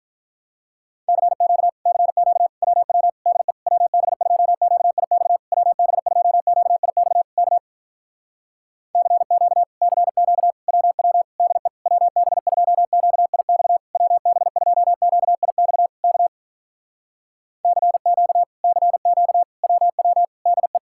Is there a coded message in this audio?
Yes: Morse code from 1.1 to 20.8 s, heard as a keyed tone, about 46 wpm, 700 Hz. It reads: CQ CQ WW DE WB2QIX WB2QIX K CQ CQ WW DE WB2QIX WB2QIX K CQ CQ WW DE